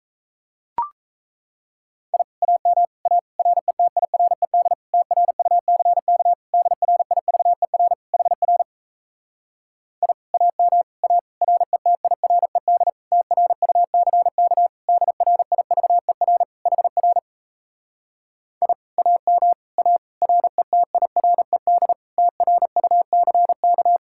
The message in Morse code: I AM A RETIRED TRUCK DRIVER HR I AM A RETIRED TRUCK DRIVER HR I AM A RETIRED TRUCK

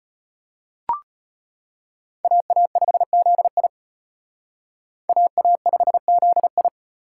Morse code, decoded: AA5ZI AA5ZI

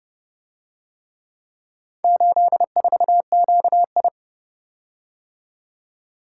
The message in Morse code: 84QI

